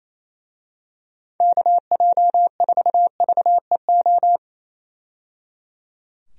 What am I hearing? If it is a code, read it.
Morse code, heard as a keyed tone, about 28 wpm, 700 Hz. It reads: KJ4VEO
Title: USA Call Signs - 6 Characters - 28wpm